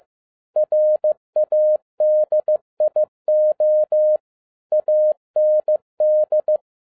Morse code, heard as a keyed tone, about 15 wpm, 610 Hz. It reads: RADIO AND
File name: 2012-01-mp3 / n07-ARA JR0DIL-15wpm